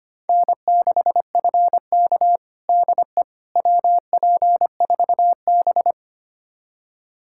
Morse code, heard as a keyed tone, about 25 wpm, 700 Hz. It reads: N6FK DE WP4B
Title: USA Callsign DE Callsign - 4 Characters - 25wpm